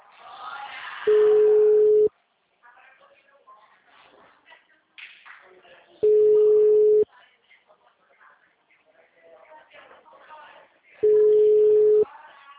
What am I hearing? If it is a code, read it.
Ringback tone (Europe, 425 Hz), with speech.